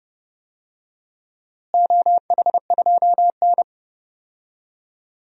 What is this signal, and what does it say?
Morse code, heard as a keyed tone, about 32 wpm, 700 Hz. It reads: OH2N
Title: DXCC-100 Call Signs - 4 Characters - Repeated 2x - 30wpm